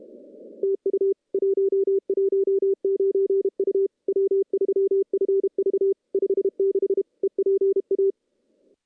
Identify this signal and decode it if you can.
Morse code, heard as a keyed tone, about 30 wpm, 390 Hz. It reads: TU 119 U W3FV 56 EPA
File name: contests / CW-SS-k5tr- -0100